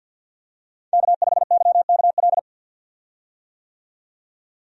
Morse code, heard as a keyed tone, about 48 wpm, 700 Hz. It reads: K5YKL
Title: USA Call Signs - 5 Characters - 50wpm